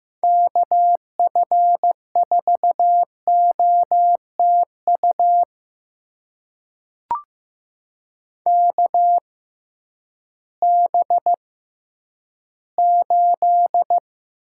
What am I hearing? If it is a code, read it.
Morse code, heard as a keyed tone, about 15 wpm, spaced as for 10 wpm, 700 Hz. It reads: KF4OTU KB8
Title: USA Call Signs - 6 Characters - 6x Character Spacing - 15wpm